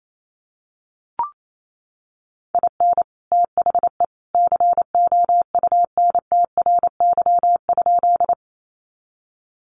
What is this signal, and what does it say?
Morse code, heard as a keyed tone, about 28 wpm, 700 Hz. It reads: IN THE COUNTRY?
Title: Sentences from Top 300 Words 28wpm